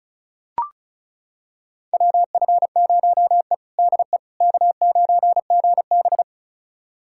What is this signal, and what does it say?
Morse code, heard as a keyed tone, about 36 wpm, 700 Hz. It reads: WF0E DE K9GB
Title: USA Callsign DE Callsign - 4 Characters - 35wpm